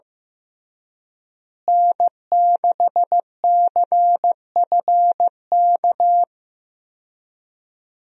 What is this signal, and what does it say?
Morse code, heard as a keyed tone, about 16 wpm, 700 Hz. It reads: N6CFK
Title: USA Call Signs - 5 Characters - Repeated 3x - 15wpm